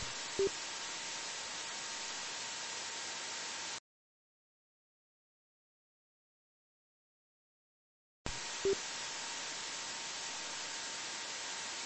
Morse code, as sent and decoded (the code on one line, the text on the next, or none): none